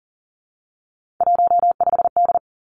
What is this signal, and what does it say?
Morse code, heard as a keyed tone, about 39 wpm, 700 Hz. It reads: 15D
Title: Three Challenging Letter-Number Combinations 40wpm